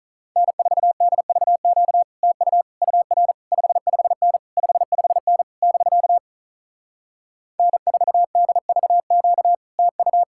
Morse code, heard as a keyed tone, about 46 wpm, 700 Hz. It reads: N4DVQ TU UR 55N 55N <BK> N4DVQ TU